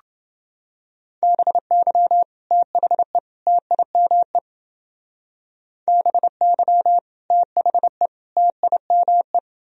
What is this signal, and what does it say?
Morse code, heard as a keyed tone, about 28 wpm, 700 Hz. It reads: BY THE TIME BY THE TIME